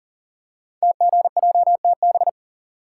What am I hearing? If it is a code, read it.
Morse code, heard as a keyed tone, about 42 wpm, 700 Hz. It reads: TGJTB
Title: Code Groups - 5 Letters - 40wpm